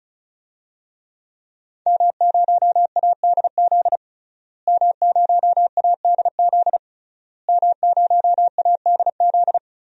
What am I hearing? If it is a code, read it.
Morse code, heard as a keyed tone, about 34 wpm, 700 Hz. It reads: M0ADZ M0ADZ M0ADZ
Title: DXCC-100 Call Signs - 5 Characters - Repeated 3x - 35wpm